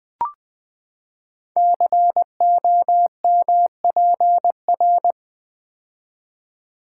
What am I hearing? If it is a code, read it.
Morse code, heard as a keyed tone, about 20 wpm, 700 Hz. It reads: COMPR
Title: Build A Word - 5 Letters At A Time - 10-Letter Words - 20wpm